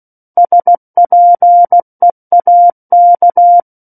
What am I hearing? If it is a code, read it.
Morse code, heard as a keyed tone, about 16 wpm, 700 Hz. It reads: SPEAK
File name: Book One / 07-Chapter 7 - 16 WPM